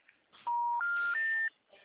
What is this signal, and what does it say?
Special information tone.